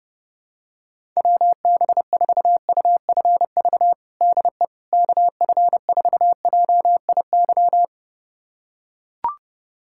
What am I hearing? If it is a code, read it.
Morse code, heard as a keyed tone, about 30 wpm, 700 Hz. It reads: WB4UFV DE KF4JIY
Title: USA Callsign DE Callsign - 6 Characters - 30wpm